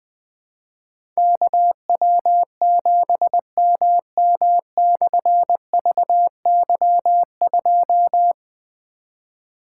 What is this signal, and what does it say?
Morse code, heard as a keyed tone, about 21 wpm, 700 Hz. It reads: KW7MM/VY2
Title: Difficult Call Signs 20wpm